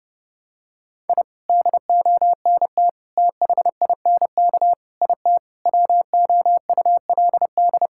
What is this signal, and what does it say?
Morse code, heard as a keyed tone, about 30 wpm, 700 Hz. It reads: I DONT THINK IT WOULD
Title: Sentences from Top 1000 Words - Review Part 6 - 30wpm